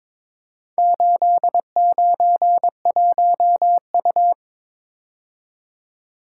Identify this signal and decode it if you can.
Morse code, heard as a keyed tone, about 22 wpm, 700 Hz. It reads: 891U
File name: practice / 22wpm / Four Letter-Number Combinations - 22wpm